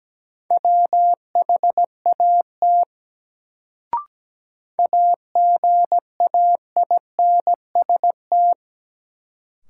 Morse code, heard as a keyed tone, about 17 wpm, 700 Hz. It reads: WHAT AGAINST
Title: Sets of 2 Words - New Word Drill - 17wpm